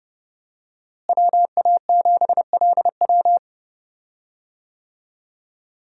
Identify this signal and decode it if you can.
Morse code, heard as a keyed tone, about 30 wpm, 700 Hz. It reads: WA7LW
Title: USA Call Signs - 5 Characters - Repeated 3x - 30wpm